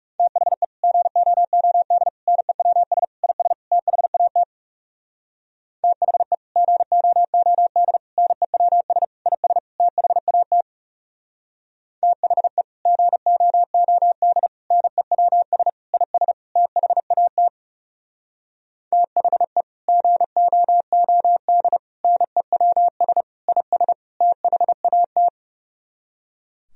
Morse code, heard as a keyed tone, about 36 wpm, 700 Hz. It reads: THE GOOD NEWS IS THAT THE GOOD NEWS IS THAT THE GOOD NEWS IS THAT THE GOOD NEWS IS THAT
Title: Speed-Racing - Sets of 5 Words 30wpm